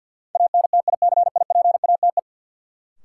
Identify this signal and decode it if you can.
Morse code, heard as a keyed tone, about 53 wpm, 700 Hz. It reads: ANTICIPATE
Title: Sentences from Top 1500 Words - New Word Drill Alt - 50wpm